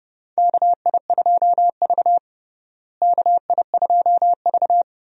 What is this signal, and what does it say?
Morse code, heard as a keyed tone, about 31 wpm, 700 Hz. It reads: KI2V KI2V
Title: USA Call Signs - 4 Characters - Repeated 3x - 30wpm